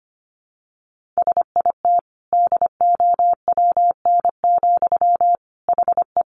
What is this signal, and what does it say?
Morse code, heard as a keyed tone, about 25 wpm, 700 Hz. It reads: SIT DOWN, HE
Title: Sentences from Top 1500 Words 25wpm